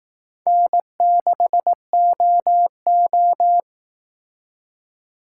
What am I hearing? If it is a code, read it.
Morse code, heard as a keyed tone, about 18 wpm, 700 Hz. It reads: N6OO
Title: Speed Racing - USA 1x2 Call Signs 15wpm